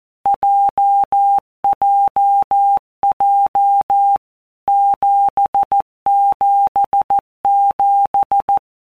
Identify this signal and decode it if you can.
Morse code, heard as a keyed tone, about 14 wpm, 800 Hz. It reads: JJJ 777
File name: Basic Letter Groups By WPM's / Groups@13 / JZ17@13